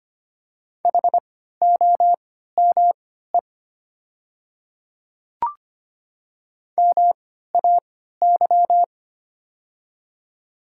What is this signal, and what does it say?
Morse code, heard as a keyed tone, about 25 wpm, spaced as for 14 wpm, 690 Hz. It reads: HOME MAY
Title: 3x Character Spacing - Top 500 Words - 25wpm